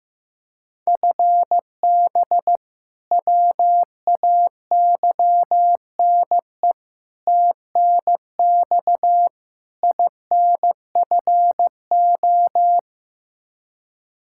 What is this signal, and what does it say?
Morse code, heard as a keyed tone, about 15 wpm, 700 Hz. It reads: FB WAYNE TNX INFO
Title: Practice QSO - Exchange 3 - More Info - 15wpm